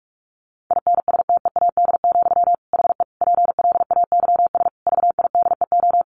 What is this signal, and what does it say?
Morse code, heard as a keyed tone, about 44 wpm, 700 Hz. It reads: INSTEAD, HE PLAYS VIDEO
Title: Additional Easy Phrases 45wpm